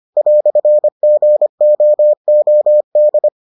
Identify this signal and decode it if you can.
Morse code, heard as a keyed tone, about 25 wpm, 600 Hz. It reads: "GOOD